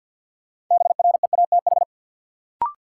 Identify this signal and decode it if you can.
Morse code, heard as a keyed tone, about 51 wpm, 700 Hz. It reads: BREATH
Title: Sets of 5 Words Encore - New Word Drill - 50wpm